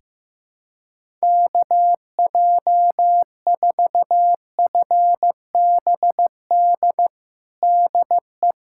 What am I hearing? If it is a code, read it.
Morse code, heard as a keyed tone, about 15 wpm, 700 Hz. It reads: KJ4FBD DE
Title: USA Callsign DE Callsign - 6 Characters - 15wpm